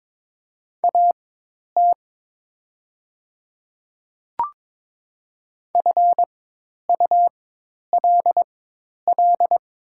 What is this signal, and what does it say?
Morse code, heard as a keyed tone, about 23 wpm, spaced as for 10 wpm, 700 Hz. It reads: AT FULL
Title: 4x Character Spacing - Top 500 Words - 22wpm